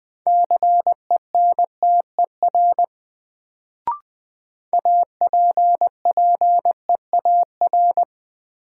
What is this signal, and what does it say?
Morse code, heard as a keyed tone, about 20 wpm, 700 Hz. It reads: CENTER APPEAR
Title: Sentences from Top 1000 Words - Review Part 2 - New Word Drill - 20wpm